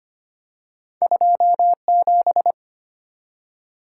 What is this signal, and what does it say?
Morse code, heard as a keyed tone, about 25 wpm, 700 Hz. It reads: 27
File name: Two Letter-Number Combinations - 25wpm